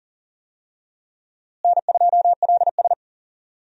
Morse code, heard as a keyed tone, about 40 wpm, 700 Hz. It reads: N2LS
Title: USA Call Signs - 4 Characters - Repeated 2x - 40wpm